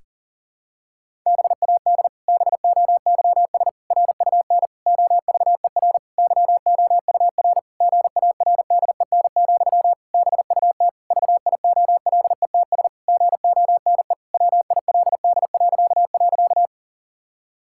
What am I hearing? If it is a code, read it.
Morse code, heard as a keyed tone, about 37 wpm, 700 Hz. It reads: BAD BOYS RUN OVER YOUR GARDEN, BUT VIOLETS GONE WILD..
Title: Pangram Challenge v2 - 40wpm